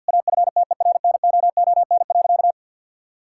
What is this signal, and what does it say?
Morse code, heard as a keyed tone, about 51 wpm, 700 Hz. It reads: AFTERNOON.
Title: Sentences from Top 1600 Words 50wpm